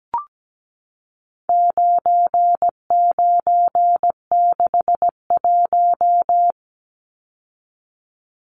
Morse code, heard as a keyed tone, about 17 wpm, 700 Hz. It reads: 9961
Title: Numbers - 4-Digit 17wpm